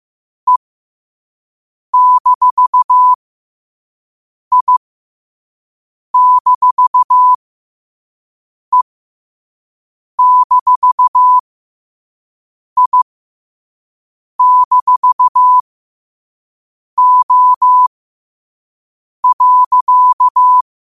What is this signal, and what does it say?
Morse code, heard as a keyed tone, about 15 wpm, spaced as for 5 wpm, 1000 Hz. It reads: E-I-E-I-O.
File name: qotdfiles / 5wpm